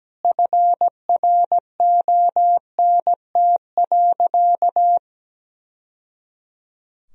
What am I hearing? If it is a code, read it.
Morse code, heard as a keyed tone, about 17 wpm, 700 Hz. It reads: FRONT.
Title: Sentences from Top 1000 Words - Review Part 2 - 17wpm